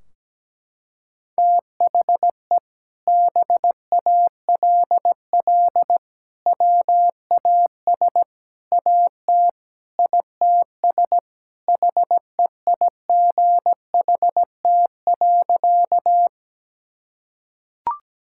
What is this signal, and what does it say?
Morse code, heard as a keyed tone, about 17 wpm, 700 Hz. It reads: THE BALL WAS AT ITS HEIGHT.
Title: Sentences from Top 1600 Words 17wpm